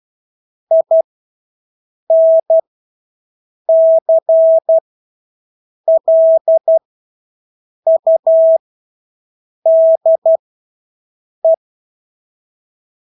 Morse code, heard as a keyed tone, about 12 wpm, spaced as for 6 wpm, 650 Hz. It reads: INCLUDE
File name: Engelske ord 12-6 wpm